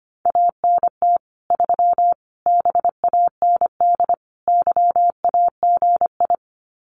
Morse code, heard as a keyed tone, about 25 wpm, 700 Hz. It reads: ANT 3 BAND YAGI
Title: Common QSO Phrases 25wpm